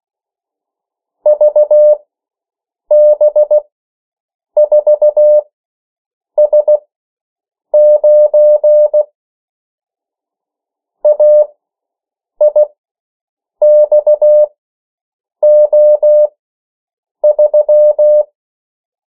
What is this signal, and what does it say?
Morse code, heard as a keyed tone, about 16 wpm, spaced as for 7 wpm, 600 Hz. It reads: VB4S9 AIXO3
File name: morse mp3 / 14 urok 13